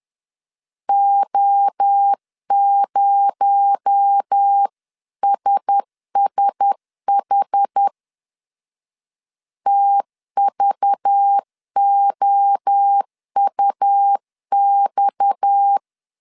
Morse code, heard as a keyed tone, about 10 wpm, 800 Hz. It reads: O0SSH TVOUX
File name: CW / DET / 1 6 1500C S C 10wpm 4252